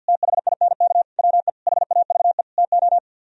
Morse code, heard as a keyed tone, about 64 wpm, 690 Hz. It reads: THINK WE HAVE TO